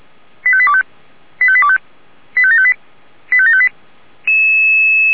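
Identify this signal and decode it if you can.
Five-tone selective call (ZVEI), heard as quick runs of five tones, tones of about 70 ms. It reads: 85825 85825 85758 85758
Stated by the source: ZVEI five-tone selective call received via SDR.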